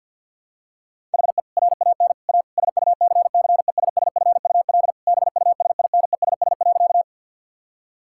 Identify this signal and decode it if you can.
Morse code, heard as a keyed tone, about 51 wpm, 700 Hz. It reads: HE RAN A SUCCESSFUL BUSINESS.
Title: Sentences from Top 1000 Words - Review Part 6 - 50wpm